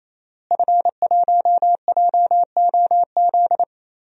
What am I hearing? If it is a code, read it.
Morse code, heard as a keyed tone, about 27 wpm, 700 Hz. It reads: F1JOZ
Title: DXCC-100 Call Signs - 5 Characters - Repeated 3x - 28wpm